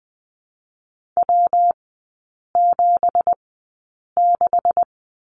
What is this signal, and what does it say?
Morse code, heard as a keyed tone, about 20 wpm, spaced as for 8 wpm, 700 Hz. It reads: W76
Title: 166 Lesson 27e - Number 7 - Three Characters 20wpm